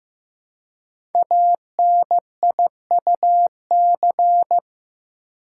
Morse code, heard as a keyed tone, about 15 wpm, 700 Hz. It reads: ANIUC
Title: Code Groups - 5 Letter:Numbers - 15wpm